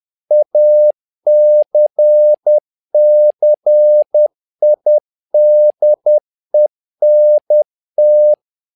Morse code, heard as a keyed tone, about 10 wpm, 600 Hz. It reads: ACCIDENT
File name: learning center / tower trouble / tower trouble 10wpm